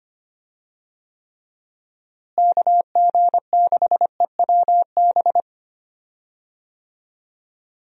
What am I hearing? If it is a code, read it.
Morse code, heard as a keyed tone, about 25 wpm, 700 Hz. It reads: KG6EWB
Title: USA Call Signs - 6 Characters - Repeated 3x - 25wpm